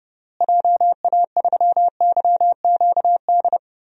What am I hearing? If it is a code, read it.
Morse code, heard as a keyed tone, about 30 wpm, 700 Hz. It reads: JA3YQD
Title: DXCC-100 Call Signs - 6 Characters - Repeated 3x - 30wpm